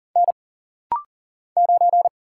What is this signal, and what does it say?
Morse code, heard as a keyed tone, about 40 wpm, 700 Hz. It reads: N 9
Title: Single Letter-Number - Rapid-Fire 40wpm